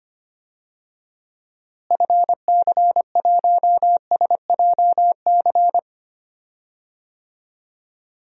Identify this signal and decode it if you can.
Morse code, heard as a keyed tone, about 25 wpm, 700 Hz. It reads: FC1SJC
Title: DXCC-100 Call Signs - 6 Characters - Repeated 3x - 25wpm